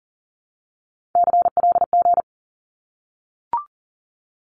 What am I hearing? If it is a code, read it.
Morse code, heard as a keyed tone, about 40 wpm, 700 Hz. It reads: CLG
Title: Abbreviations 40wpm